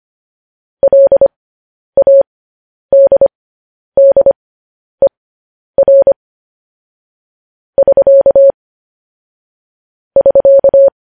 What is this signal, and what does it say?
Morse code, heard as a keyed tone, about 25 wpm, spaced as for 10 wpm, 550 Hz. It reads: LADDER <SK> <SK>
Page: News Headlines in Morse Code at 10 WPM